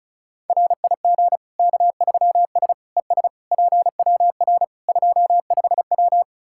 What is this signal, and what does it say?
Morse code, heard as a keyed tone, about 34 wpm, 700 Hz. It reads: RIG K3S ES PWR 25W
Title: Practice QSO - Exchange 2 - Info - 35wpm